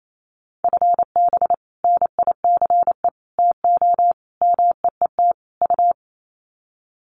Morse code, heard as a keyed tone, about 28 wpm, 700 Hz. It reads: FB NICE TO MEET U
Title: Common QSO Phrases 28wpm